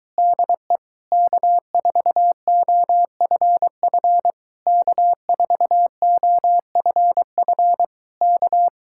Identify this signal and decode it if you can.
Morse code, heard as a keyed tone, about 23 wpm, 700 Hz. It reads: DE K4OFF K4OFF K